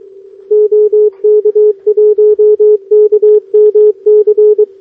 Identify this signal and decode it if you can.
Morse code, heard as a keyed tone, about 22 wpm, 420 Hz. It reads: OK1KMC
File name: OK QRP 2026 / OKQRP2026